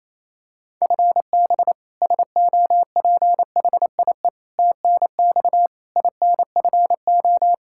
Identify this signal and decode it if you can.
Morse code, heard as a keyed tone, about 28 wpm, 700 Hz. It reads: FB SOPHIE TNX INFO
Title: Practice QSO - Exchange 3 - More Info - 28wpm